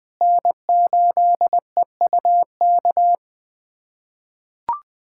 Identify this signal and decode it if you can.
Morse code, heard as a keyed tone, about 20 wpm, 700 Hz. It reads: N8EUK
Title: USA Callsign DE Callsign - 5 Characters - 20wpm